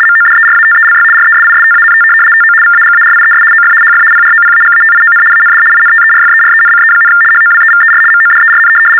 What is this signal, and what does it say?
RTTY, about 75 baud. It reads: MODERATELY RESISTANT TO VAGARIES OF HF PROPAGATION AND INTERFERENCE, HOWEVER MODERN
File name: teletype-RTTY-75